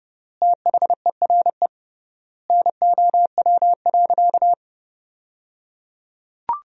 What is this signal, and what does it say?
Morse code, heard as a keyed tone, about 29 wpm, 700 Hz. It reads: THERE NOW.
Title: 3x Word-Spacing - Sentences from Top 200 Words - 30wpm